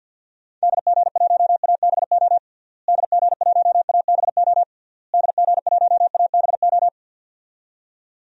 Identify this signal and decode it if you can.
Morse code, heard as a keyed tone, about 50 wpm, 700 Hz. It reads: DG1ABO DG1ABO DG1ABO
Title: DXCC-100 Call Signs - 6 Characters - Repeated 3x - 50wpm